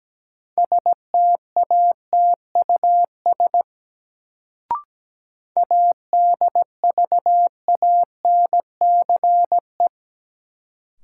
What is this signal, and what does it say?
Morse code, heard as a keyed tone, about 17 wpm, 700 Hz. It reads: STATUS ADVANCE
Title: Sentences from Top 1400 Words - New Word Drill - 17wpm